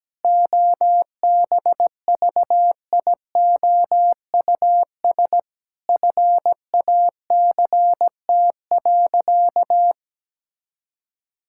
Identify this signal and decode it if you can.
Morse code, heard as a keyed tone, about 17 wpm, 700 Hz. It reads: OBVIOUS FACT.
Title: Sentences from Top 1600 Words 17wpm